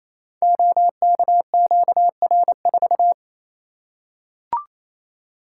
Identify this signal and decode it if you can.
Morse code, heard as a keyed tone, about 28 wpm, 700 Hz. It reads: OKQR4
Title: Code Groups - 5 Letter:Numbers - 28wpm